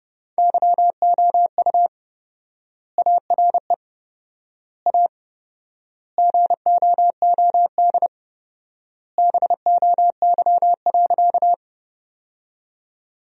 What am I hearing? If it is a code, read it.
Morse code, heard as a keyed tone, about 30 wpm, 700 Hz. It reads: YOU ARE A GOOD BOY.
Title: 4x Word-Spacing - Sentences from Top 200 Words - 30wpm